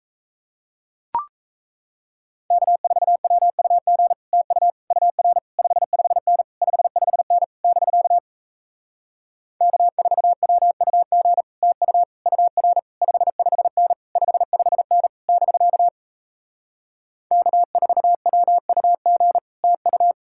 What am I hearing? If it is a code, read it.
Morse code, heard as a keyed tone, about 37 wpm, 700 Hz. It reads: K4WUG TU UR 55N 55N <BK> K4WUG TU UR 55N 55N <BK> K4WUG TU